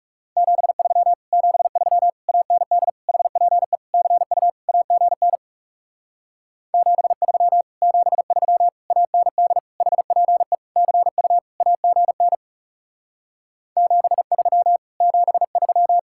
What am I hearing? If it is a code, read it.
Morse code, heard as a keyed tone, about 40 wpm, 700 Hz. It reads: 73 73 AND HPE CU AGN 73 73 AND HPE CU AGN 73 73